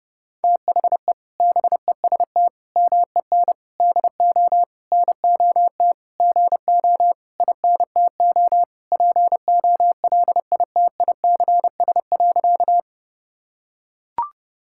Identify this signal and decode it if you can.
Morse code, heard as a keyed tone, about 32 wpm, 700 Hz. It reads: THE BEST MEN DO NOT GO INTO POLITICS.
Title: Sentences from Top 1000 Words - Review Part 6 - 30wpm